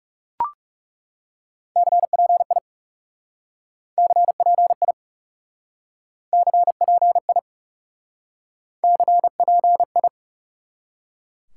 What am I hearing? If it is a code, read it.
Morse code, heard as a keyed tone, about 37 wpm, 700 Hz. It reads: CPI CPI CPI CPI
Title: Speed-Racing Top 180 QSO Elements 30wpm